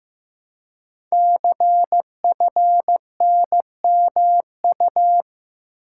Morse code, heard as a keyed tone, about 15 wpm, 700 Hz. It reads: CFNMU
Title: Code Groups - 5 Letters - 15wpm